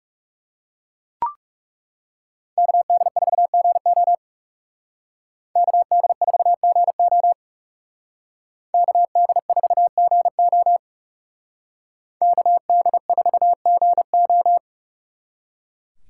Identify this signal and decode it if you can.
Morse code, heard as a keyed tone, about 37 wpm, 700 Hz. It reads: KD4GO KD4GO KD4GO KD4GO
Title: Speed Racing - USA 2x2 Call Signs 30wpm